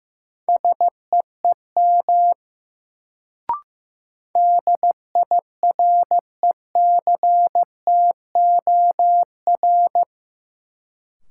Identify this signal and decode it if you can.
Morse code, heard as a keyed tone, about 15 wpm, 700 Hz. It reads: SEEM DIRECTOR
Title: Sets of 3 Words - New Word Drill - 15wpm